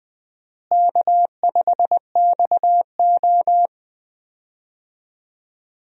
Morse code, heard as a keyed tone, about 20 wpm, 700 Hz. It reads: K5XO
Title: USA Call Signs - 4 Characters - Repeated 2x - 20wpm